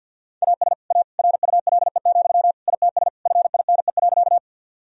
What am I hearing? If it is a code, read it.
Morse code, heard as a keyed tone, about 50 wpm, 700 Hz. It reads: AS A RULE, ITS FINE.
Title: Sentences from Top 1000 Words - Review Part 2 - 50wpm